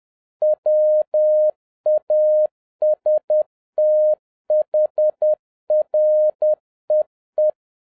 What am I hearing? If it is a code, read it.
Morse code, heard as a keyed tone, about 10 wpm, 610 Hz. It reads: WAS THREE
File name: 2011-07-mp3 / n56 BACK FROM A BIG ACCIDENT 10wpm